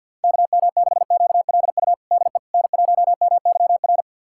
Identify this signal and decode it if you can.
Morse code, heard as a keyed tone, about 50 wpm, 700 Hz. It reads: KM6QLU DE N1MYR
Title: USA Callsign DE Callsign 50wpm